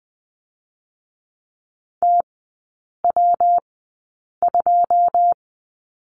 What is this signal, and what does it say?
Morse code, heard as a keyed tone, about 20 wpm, spaced as for 8 wpm, 700 Hz. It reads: TW2
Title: 121 Lesson 20e - Number 3 - Three Characters 20wpm